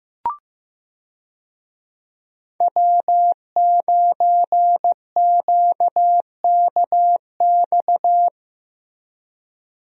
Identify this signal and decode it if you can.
Morse code, heard as a keyed tone, about 15 wpm, 700 Hz. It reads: W9QKX
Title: USA Call Signs - 5 Characters - Repeated 2x - 15wpm